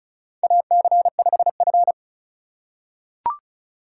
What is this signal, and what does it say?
Morse code, heard as a keyed tone, about 34 wpm, 700 Hz. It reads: AC5F
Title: USA Call Signs - 4 Characters - 4x Character Spacing - 35wpm